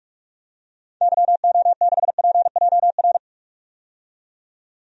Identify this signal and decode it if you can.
Morse code, heard as a keyed tone, about 46 wpm, 700 Hz. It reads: YO6PJR
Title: DXCC-100 Call Signs - 6 Characters - 45wpm